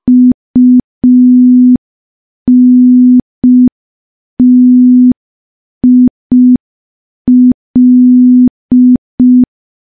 Morse code, heard as a keyed tone, about 5 wpm, 250 Hz. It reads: UNTIL
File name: morse / morse-training-05WPM